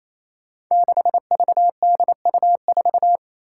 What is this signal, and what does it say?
Morse code, heard as a keyed tone, about 27 wpm, 700 Hz. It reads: 6VDU4
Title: Five Challenging Letter-Number Combinations 28wpm